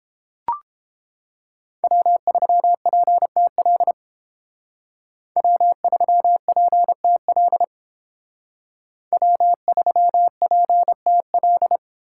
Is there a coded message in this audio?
Yes: Morse code from 1.8 to 11.8 s, heard as a keyed tone, about 30 wpm, 700 Hz. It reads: W3PTL W3PTL W3PTL